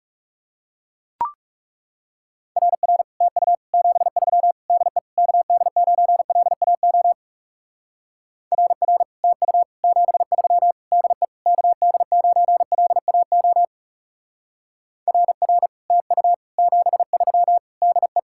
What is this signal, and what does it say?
Morse code, heard as a keyed tone, about 40 wpm, 700 Hz. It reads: RR TU 73 DE KD9LAO RR TU 73 DE KD9LAO RR TU 73 DE